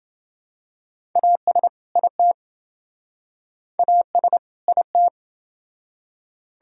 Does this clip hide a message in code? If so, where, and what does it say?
Yes: Morse code from 1.2 to 5.1 s, heard as a keyed tone, about 28 wpm, 700 Hz. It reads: AS IT AS IT